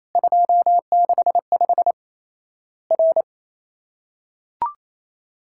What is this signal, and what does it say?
Morse code, heard as a keyed tone, about 28 wpm, 700 Hz. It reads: 265 R
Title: Worked All Europe DX Contest - QTC Exchange - 28wpm